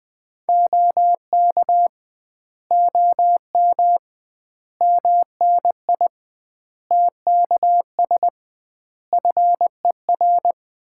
Morse code, heard as a keyed tone, about 20 wpm, 700 Hz. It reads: OK OM MNI TKS FER
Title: Common QSO Phrases - 2x Word-Spacing - 20wpm